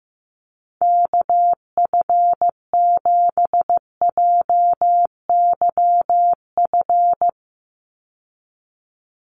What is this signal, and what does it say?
Morse code, heard as a keyed tone, about 15 wpm, 700 Hz. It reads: KF7JYF
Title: USA Call Signs 15wpm